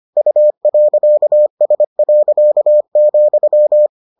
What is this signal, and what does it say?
Morse code, heard as a keyed tone, about 25 wpm, 600 Hz. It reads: U.S.,